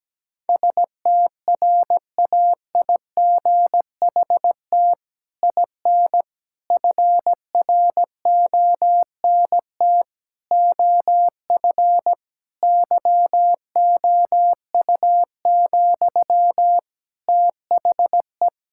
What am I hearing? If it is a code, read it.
Morse code, heard as a keyed tone, about 17 wpm, 700 Hz. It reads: STRAIGHT IN FRONT OF YOU, THE
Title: Sentences from Top 1000 Words - Review Part 6 - 17wpm